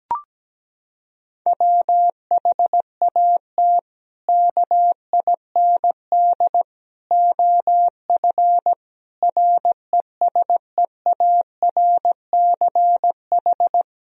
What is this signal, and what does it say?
Morse code, heard as a keyed tone, about 17 wpm, 700 Hz. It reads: WHAT KIND OF RESEARCH
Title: Sentences from Top 1000 Words - Review Part 4 - 17wpm